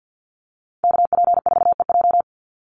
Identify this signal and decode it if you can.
Morse code, heard as a keyed tone, about 50 wpm, 700 Hz. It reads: KP4EJ
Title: Call Signs - 50wpm